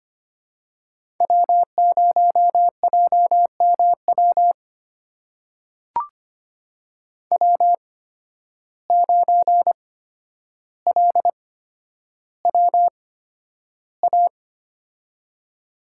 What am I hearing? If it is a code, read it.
Morse code, heard as a keyed tone, about 25 wpm, spaced as for 11 wpm, 700 Hz. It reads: W0JMW W9LWA
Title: USA Call Signs - 8x Character Spacing - 25wpm